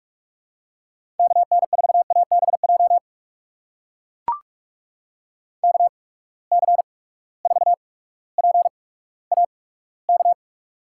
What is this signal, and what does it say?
Morse code, heard as a keyed tone, about 43 wpm, spaced as for 19 wpm, 700 Hz. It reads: KN4ABJ KC4PAK
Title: USA Call Signs - 6 Characters - 8x Character Spacing - 45wpm